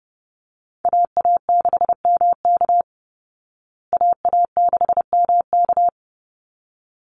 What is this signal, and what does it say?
Morse code, heard as a keyed tone, about 28 wpm, 700 Hz. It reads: AA6MK AA6MK
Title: Call Signs Repeated 3x - 30wpm